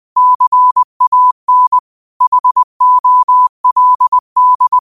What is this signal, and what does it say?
Morse code, heard as a keyed tone, about 20 wpm, 1000 Hz. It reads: CAN HOLD
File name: qotdfiles / 20wpm